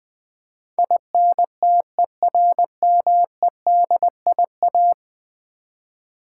Morse code, heard as a keyed tone, about 20 wpm, 700 Hz. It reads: INTERMEDIA
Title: Build A Word - 5 Letters At A Time - 12-Letter Words - 20wpm